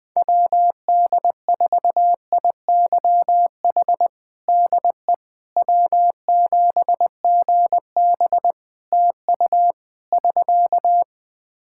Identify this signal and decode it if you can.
Morse code, heard as a keyed tone, about 20 wpm, 700 Hz. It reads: WD4IYH DE W7GB TU <SK>
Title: Practice QSO - Exchange 4 - Ending - 20wpm